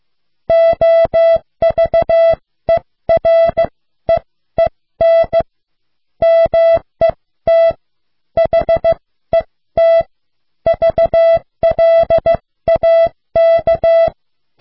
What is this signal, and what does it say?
Morse code, heard as a keyed tone, about 16 wpm, spaced as for 13 wpm, 660 Hz. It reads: OVEREEN MET HET VLAK